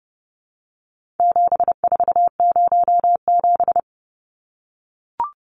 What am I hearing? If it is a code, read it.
Morse code, heard as a keyed tone, about 28 wpm, 700 Hz. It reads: 7407
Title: Numbers - 4-Digit 30wpm